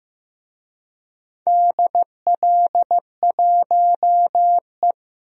Morse code, heard as a keyed tone, about 15 wpm, 700 Hz. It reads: DL1E
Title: DXCC-100 Call Signs - 4 Characters - Repeated 2x - 15wpm